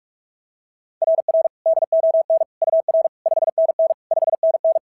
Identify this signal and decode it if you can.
Morse code, heard as a keyed tone, about 44 wpm, 660 Hz. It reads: RR DON UR 5NN 5NN
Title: SOTA QSO Practice 45wpm